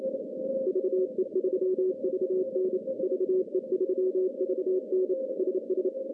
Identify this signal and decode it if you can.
Morse code, heard as a keyed tone, about 28 wpm, 370 Hz, with noise. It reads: VE3VN VE3VN SS
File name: contests / CW-SS-k5tr- -1200